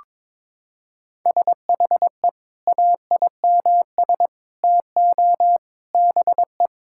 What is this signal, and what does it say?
Morse code, heard as a keyed tone, about 22 wpm, 700 Hz. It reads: SHE AIMS TO BE
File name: Sentences with 9-Word Phrases - 22wpm